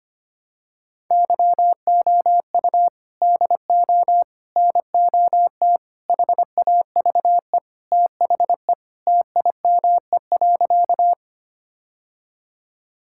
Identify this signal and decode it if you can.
Morse code, heard as a keyed tone, about 25 wpm, 700 Hz. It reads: YOU DO NOT HAVE THE TIME.
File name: practice / 25wpm / Sentences with 4-Word Phrases - 25wpm